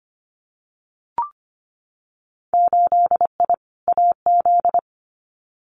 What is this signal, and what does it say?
Morse code, heard as a keyed tone, about 25 wpm, 700 Hz. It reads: 8I AZ
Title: Winter Field Day Exchanges Encore 25wpm